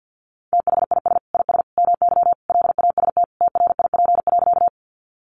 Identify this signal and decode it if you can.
Morse code, heard as a keyed tone, about 50 wpm, 700 Hz. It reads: THIS IS MY LAST TRIP.
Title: Sentences from Top 1300 Words 50wpm